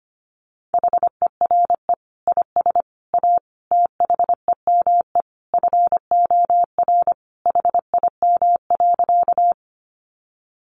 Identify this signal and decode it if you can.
Morse code, heard as a keyed tone, about 25 wpm, 700 Hz. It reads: HERE IS A THEME FOR HIM.
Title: Sentences from Top 1100 Words 25wpm